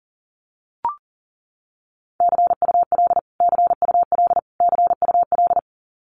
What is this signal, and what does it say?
Morse code, heard as a keyed tone, about 40 wpm, 700 Hz. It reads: CUL CUL CUL
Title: Abbreviations Repeated 3x - 40wpm